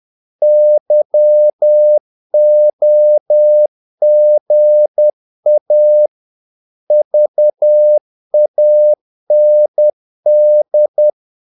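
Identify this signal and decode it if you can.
Morse code, heard as a keyed tone, about 10 wpm, 600 Hz. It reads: YOGA VAND
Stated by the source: Morse practice of short Danish words at 10 wpm, from CW med Gnister.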